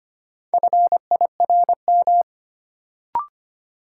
Morse code, heard as a keyed tone, about 25 wpm, 700 Hz. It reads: FIRM
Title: Sentences from Top 1400 Words - New Word Drill - 25wpm